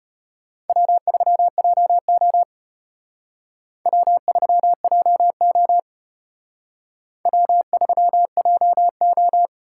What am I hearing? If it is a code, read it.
Morse code, heard as a keyed tone, about 33 wpm, 700 Hz. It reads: W3JO W3JO W3JO